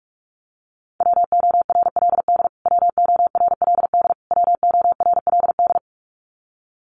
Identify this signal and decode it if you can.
Morse code, heard as a keyed tone, about 44 wpm, 700 Hz. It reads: WORLD WORLD WORLD
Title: Top 500 Words Repeated 45wpm